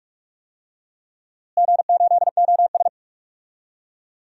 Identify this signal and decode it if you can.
Morse code, heard as a keyed tone, about 42 wpm, 700 Hz. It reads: G8OS
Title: DXCC-100 Call Signs - 4 Characters - Repeated 2x - 45wpm